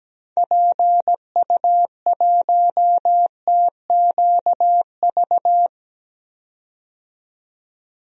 Morse code, heard as a keyed tone, about 17 wpm, 700 Hz. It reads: PU1TQV
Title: DXCC-100 Call Signs - 6 Characters - 17wpm